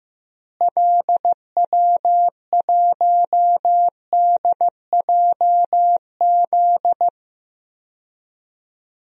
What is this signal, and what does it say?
Morse code, heard as a keyed tone, about 15 wpm, 700 Hz. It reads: LW1DJZ
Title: DXCC-100 Call Signs - 6 Characters - Repeated 2x - 15wpm